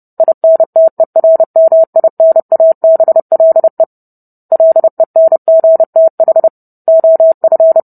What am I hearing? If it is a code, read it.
Morse code, heard as a keyed tone, about 30 wpm, 650 Hz. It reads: INTERMINABLE LENGTH OF
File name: Book Two / 04-Chapter 4 - 30 WPM